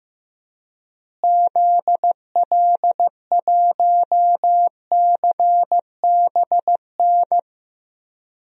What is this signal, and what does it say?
Morse code, heard as a keyed tone, about 15 wpm, 700 Hz. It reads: ZL1CBN
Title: DXCC-100 Call Signs - 6 Characters - Repeated 3x - 15wpm